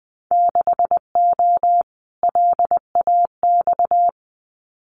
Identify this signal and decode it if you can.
Morse code, heard as a keyed tone, about 20 wpm, 700 Hz. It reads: 6O LAX
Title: Winter Field Day Exchanges Encore 20wpm